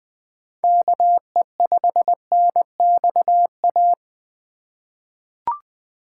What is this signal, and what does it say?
Morse code, heard as a keyed tone, about 21 wpm, 700 Hz. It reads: KE5NXA
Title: USA Call Signs - 2x Character Spacing - 20wpm